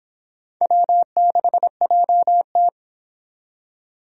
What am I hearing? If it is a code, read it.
Morse code, heard as a keyed tone, about 26 wpm, 700 Hz. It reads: W6JT